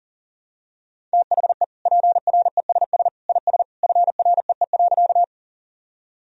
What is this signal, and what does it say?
Morse code, heard as a keyed tone, about 40 wpm, 700 Hz. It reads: THE PRESS IS FREE.
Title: Sentences from Top 1000 Words - Review Part 1 - 40wpm